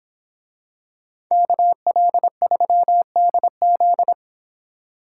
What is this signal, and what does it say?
Morse code, heard as a keyed tone, about 24 wpm, 700 Hz. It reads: KL3DZ